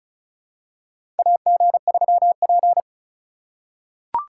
Morse code, heard as a keyed tone, about 36 wpm, 700 Hz. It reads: AG3P
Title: USA Call Signs - 4 Characters - 2x Character Spacing - 35wpm